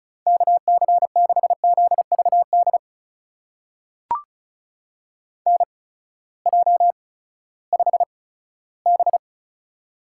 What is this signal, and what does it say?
Morse code, heard as a keyed tone, about 36 wpm, spaced as for 18 wpm, 700 Hz. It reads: KC6ZVD NJ5B
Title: USA Call Signs - 8x Character Spacing - 35wpm